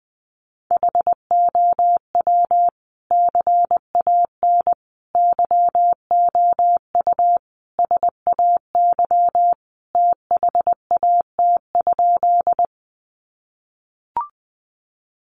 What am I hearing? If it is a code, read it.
Morse code, heard as a keyed tone, about 20 wpm, 700 Hz. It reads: HOW CAN YOU SAY THAT?
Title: Sentences from Top 200 Words 20wpm